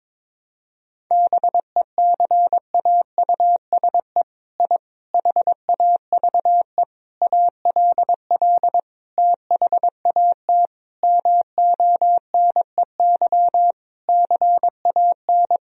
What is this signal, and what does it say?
Morse code, heard as a keyed tone, about 22 wpm, 700 Hz. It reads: BECAUSE I HAVE ALL THAT MONEY CAN
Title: Sentences from Top 1000 Words - Review Part 4 - 22wpm